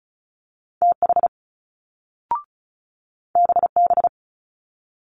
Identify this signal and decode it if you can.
Morse code, heard as a keyed tone, about 36 wpm, 700 Hz. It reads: TH BB
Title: Multi-letter Phonemes Quick Fire 35wpm